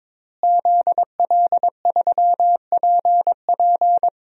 Morse code, heard as a keyed tone, about 22 wpm, 700 Hz. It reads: ZL3PP
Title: DXCC-100 Call Signs - 5 Characters - Repeated 2x - 22wpm